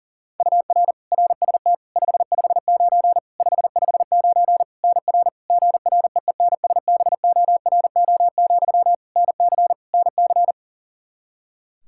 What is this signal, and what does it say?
Morse code, heard as a keyed tone, about 40 wpm, 700 Hz. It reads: UR RST 559 559 NR GREENSBORO, NC NC
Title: K1USN SST QSO Practice 40wpm